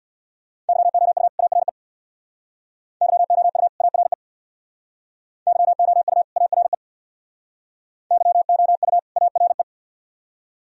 Morse code, heard as a keyed tone, about 62 wpm, 700 Hz. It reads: YOU ARE YOU ARE YOU ARE YOU ARE
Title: Speed-Racing - Sets of 2 Words 50wpm